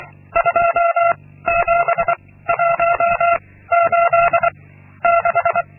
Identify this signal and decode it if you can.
Morse code, heard as a keyed tone, about 23 wpm, 680 Hz. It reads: 2 7 1 8 6
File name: Linksmakalnis-2